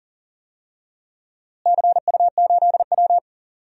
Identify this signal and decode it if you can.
Morse code, heard as a keyed tone, about 38 wpm, 700 Hz. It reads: CU8W
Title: DXCC-100 Call Signs - 4 Characters - Repeated 3x - 40wpm